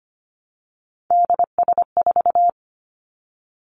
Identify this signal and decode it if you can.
Morse code, heard as a keyed tone, about 25 wpm, 700 Hz. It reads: DS4
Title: Three Challenging Letter-Number Combinations 25wpm